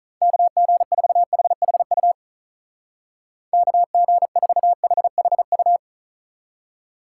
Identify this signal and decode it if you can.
Morse code, heard as a keyed tone, about 44 wpm, 700 Hz. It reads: KG4HHU KG4HHU